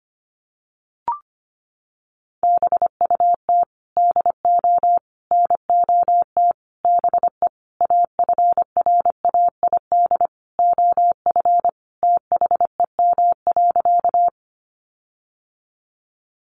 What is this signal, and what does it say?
Morse code, heard as a keyed tone, about 25 wpm, 700 Hz. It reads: BUT DO NOT BE AFRAID OF THEM.
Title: Sentences from Top 1500 Words 25wpm